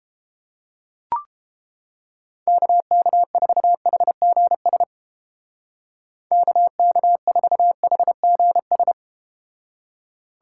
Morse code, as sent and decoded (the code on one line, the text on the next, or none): -.- -.- ....- .... --. ... / -.- -.- ....- .... --. ...
KK4HGS KK4HGS